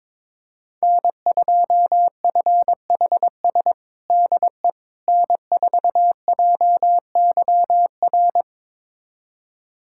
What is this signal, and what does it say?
Morse code, heard as a keyed tone, about 22 wpm, 700 Hz. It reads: N2FHS DE N4JYR
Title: USA Callsign DE Callsign - 5 Characters - 22wpm